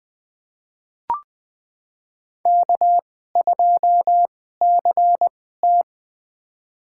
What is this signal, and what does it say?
Morse code, heard as a keyed tone, about 20 wpm, spaced as for 15 wpm, 700 Hz. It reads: K2CT
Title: USA Call Signs - 4 Characters - 2x Character Spacing - 20wpm